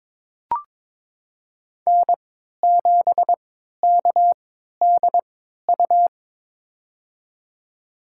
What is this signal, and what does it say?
Morse code, heard as a keyed tone, about 22 wpm, spaced as for 12 wpm, 700 Hz. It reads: N7KDU
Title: USA Call Signs - 5 Characters - 3x Character Spacing - 22wpm